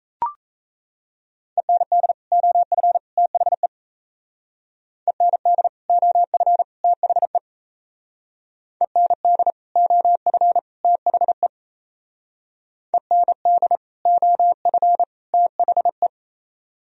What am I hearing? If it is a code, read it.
Morse code, heard as a keyed tone, about 35 wpm, 700 Hz. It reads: END OF THE END OF THE END OF THE END OF THE
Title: Speed-Racing - Sets of 3 Words 28wpm